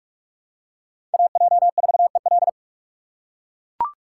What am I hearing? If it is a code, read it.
Morse code, heard as a keyed tone, about 44 wpm, 700 Hz. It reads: AJ4EL
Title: USA Call Signs - 6x Character Spacing - 45wpm